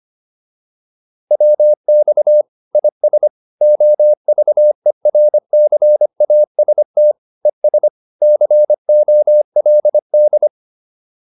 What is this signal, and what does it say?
Morse code, heard as a keyed tone, about 25 wpm, 600 Hz. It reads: WX IS OVERCAST ES COLD